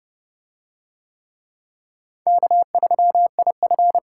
Morse code, heard as a keyed tone, about 29 wpm, 700 Hz. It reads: K3IF